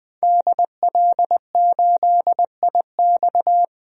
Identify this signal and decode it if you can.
Morse code, heard as a keyed tone, about 20 wpm, 700 Hz. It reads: DL8IX
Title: DXCC-100 Call Signs - 5 Characters - 20wpm